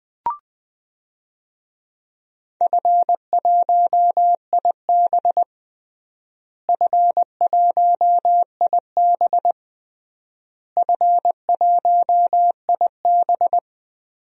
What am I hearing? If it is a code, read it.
Morse code, heard as a keyed tone, about 20 wpm, 700 Hz. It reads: F1IB F1IB F1IB
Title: DXCC-100 Call Signs - 4 Characters - Repeated 3x - 20wpm